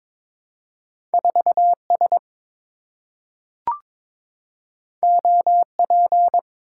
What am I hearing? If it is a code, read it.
Morse code, heard as a keyed tone, about 22 wpm, 700 Hz. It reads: 4S OP
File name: practice / 22wpm / Two Letter-Number Combinations - 22wpm